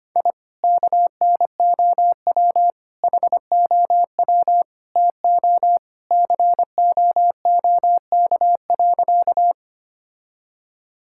Morse code, heard as a keyed tone, about 25 wpm, 700 Hz. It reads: I KNOW HOW TO COOK.
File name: practice / 25wpm / Sentences with 3-Word Phrases - 25wpm